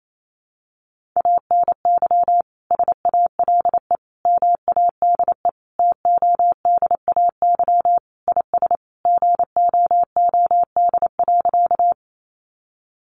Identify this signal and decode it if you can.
Morse code, heard as a keyed tone, about 28 wpm, 700 Hz. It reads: ANY SALE MADE TODAY IS GOOD.
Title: Sentences from Top 1400 Words 28wpm